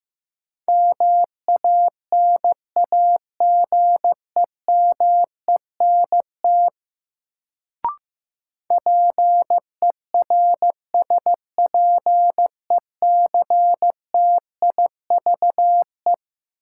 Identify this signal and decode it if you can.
Morse code, heard as a keyed tone, about 15 wpm, 700 Hz. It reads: MANAGEMENT PERSPECTIVE
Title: Sentences from Top 1200 Words - New Word Drill Alt - 15wpm